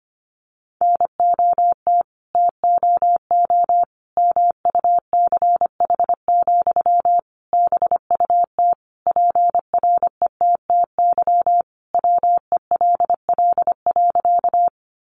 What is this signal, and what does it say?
Morse code, heard as a keyed tone, about 25 wpm, 700 Hz. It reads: NOT TOO MUCH, BUT PRETTY WELL.
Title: Sentences from Top 1000 Words 25wpm